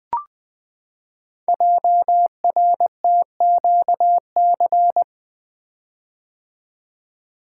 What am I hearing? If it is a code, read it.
Morse code, heard as a keyed tone, about 20 wpm, 700 Hz. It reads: JRTQC
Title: Code Groups - 5 Letters - 20wpm